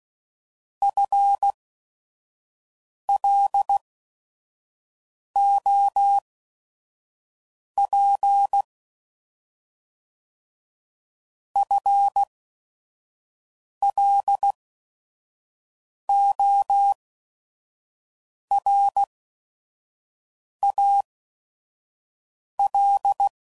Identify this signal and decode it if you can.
Morse code, heard as a keyed tone, about 16 wpm, spaced as for 5 wpm, 800 Hz. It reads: FLOP FLORAL